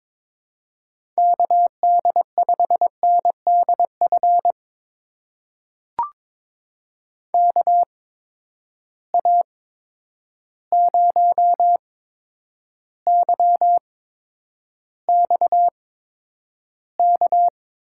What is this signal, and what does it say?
Morse code, heard as a keyed tone, about 22 wpm, spaced as for 9 wpm, 700 Hz. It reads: KD5NDF KA0YXK
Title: USA Call Signs - 6 Characters - 8x Character Spacing - 22wpm